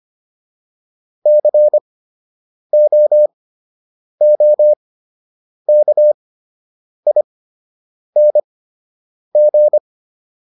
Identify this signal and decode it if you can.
Morse code, heard as a keyed tone, about 25 wpm, spaced as for 8 wpm, 600 Hz. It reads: COOKING